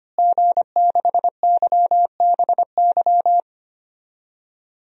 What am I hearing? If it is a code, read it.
Morse code, heard as a keyed tone, about 25 wpm, 700 Hz. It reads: G6YBY
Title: DXCC-100 Call Signs - 5 Characters - Repeated 2x - 25wpm